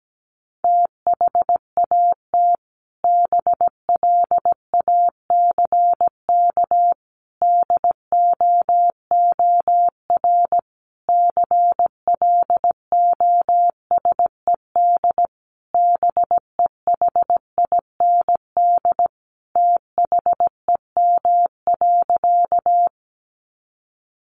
Morse code, heard as a keyed tone, about 17 wpm, 700 Hz. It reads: THAT BLACK DOOR CLOSED BEHIND THEM.
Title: Sentences from Top 1400 Words 17wpm